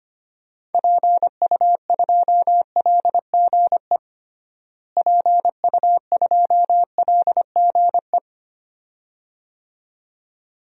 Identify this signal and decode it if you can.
Morse code, heard as a keyed tone, about 25 wpm, 700 Hz. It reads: PU2LGE PU2LGE
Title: DXCC-100 Call Signs - 6 Characters - Repeated 2x - 25wpm